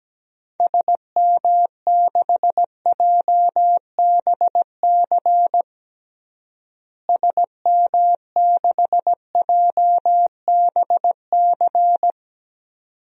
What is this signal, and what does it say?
Morse code, heard as a keyed tone, about 17 wpm, 700 Hz. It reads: SM6JBC SM6JBC
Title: DXCC-100 Call Signs - 6 Characters - Repeated 2x - 17wpm